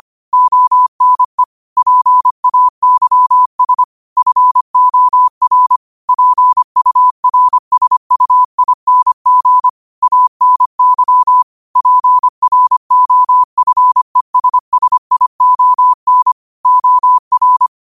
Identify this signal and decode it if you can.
Morse code, heard as a keyed tone, about 25 wpm, 1000 Hz. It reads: ONE PAYS FOR PURSUING ANY PROFESSION OR
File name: qotdfiles / 25wpm